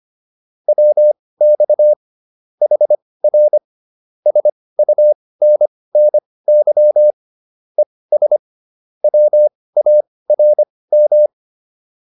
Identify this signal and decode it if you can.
Morse code, heard as a keyed tone, about 25 wpm, spaced as for 18 wpm, 600 Hz. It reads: WX HR SUNNY ES WARM
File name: QSO303 18